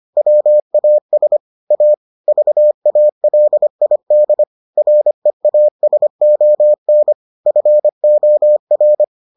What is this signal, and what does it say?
Morse code, heard as a keyed tone, about 25 wpm, 600 Hz. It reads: WAS A VALID REASON FOR